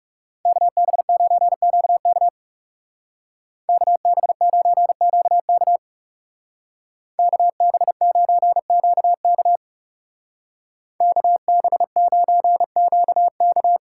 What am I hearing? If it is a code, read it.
Morse code, heard as a keyed tone, about 37 wpm, 700 Hz. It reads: KB9QK KB9QK KB9QK KB9QK
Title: Speed Racing - USA Call Signs 30wpm